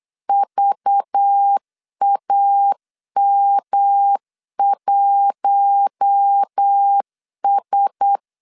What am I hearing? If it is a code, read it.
Morse code, heard as a keyed tone, about 8 wpm, 800 Hz. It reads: VAM1S
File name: LEZIONI / 6 PKX0 / 6 PK 8wpm 3003